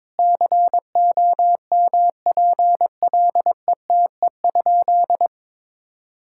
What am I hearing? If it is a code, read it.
Morse code, heard as a keyed tone, about 22 wpm, 700 Hz. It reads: COMPLETE?
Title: Sentences from Top 1000 Words - Review Part 6 - 22wpm